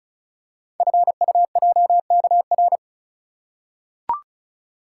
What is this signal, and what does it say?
Morse code, heard as a keyed tone, about 36 wpm, 700 Hz. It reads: FUJKR
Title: Code Groups - 5 Letters - 35wpm